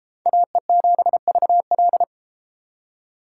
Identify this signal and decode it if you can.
Morse code, heard as a keyed tone, about 32 wpm, 700 Hz. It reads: AE7VL